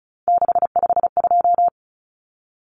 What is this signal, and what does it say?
Morse code, heard as a keyed tone, about 34 wpm, 700 Hz. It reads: 652
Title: Numbers - 3-Digit 35wpm